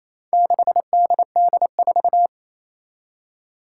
Morse code, heard as a keyed tone, about 29 wpm, 700 Hz. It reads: 6DD4
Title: Four Challenging Letter-Number Combinations 28wpm